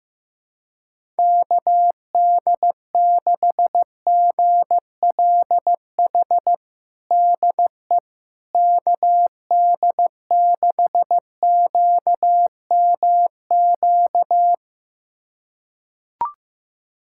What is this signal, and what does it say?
Morse code, heard as a keyed tone, about 15 wpm, 700 Hz. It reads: KD6GLH DE KD6QMQ
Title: USA Callsign DE Callsign - 6 Characters - 15wpm